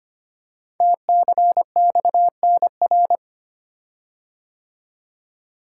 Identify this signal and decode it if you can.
Morse code, heard as a keyed tone, about 25 wpm, 700 Hz. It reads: TCXNR
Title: Code Groups - 5 Letters - 25wpm